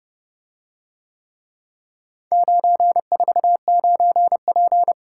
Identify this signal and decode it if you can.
Morse code, heard as a keyed tone, about 30 wpm, 700 Hz. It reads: 949P